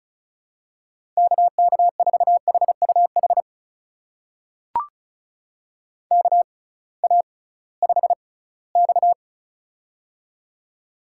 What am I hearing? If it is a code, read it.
Morse code, heard as a keyed tone, about 36 wpm, spaced as for 21 wpm, 700 Hz. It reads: KK4HUH KA5X
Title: USA Call Signs - 6x Character Spacing - 35wpm